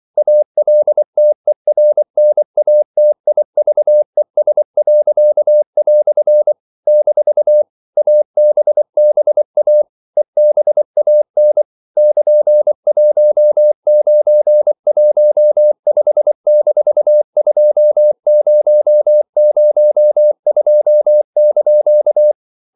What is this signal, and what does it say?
Morse code, heard as a keyed tone, about 24 wpm, 600 Hz. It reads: ALTERNATIVES." - ABBA EBAN <KN>1915-2002)
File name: CW / Practice / EnglishQuotations0424